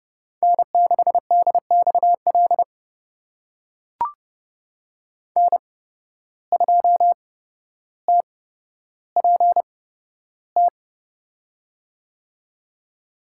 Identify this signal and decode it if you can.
Morse code, heard as a keyed tone, about 30 wpm, spaced as for 13 wpm, 700 Hz. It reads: N6DXL N2TPT
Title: USA Call Signs - 5 Characters - 8x Character Spacing - 30wpm